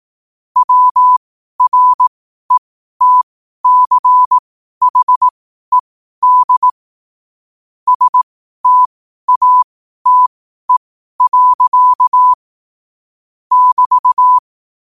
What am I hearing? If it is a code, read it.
Morse code, heard as a keyed tone, about 18 wpm, spaced as for 12 wpm, 1000 Hz. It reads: WRETCHED STATE. <BT>
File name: qotdfiles / 12wpm